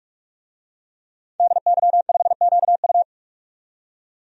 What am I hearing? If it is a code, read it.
Morse code, heard as a keyed tone, about 41 wpm, 700 Hz. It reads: DY5QU
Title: DXCC-100 Call Signs - 5 Characters - 45wpm